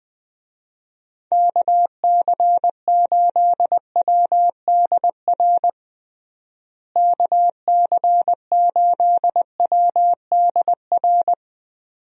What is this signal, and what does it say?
Morse code, heard as a keyed tone, about 21 wpm, 700 Hz. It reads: KC8WDR KC8WDR
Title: USA Call Signs - 6 Characters - Repeated 2x - 20wpm